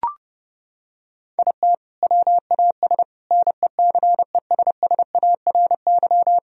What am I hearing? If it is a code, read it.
Morse code, heard as a keyed tone, about 32 wpm, 700 Hz. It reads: IT WAS NECESSARY
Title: Sentences from Top 1000 Words - Review Part 4 - 30wpm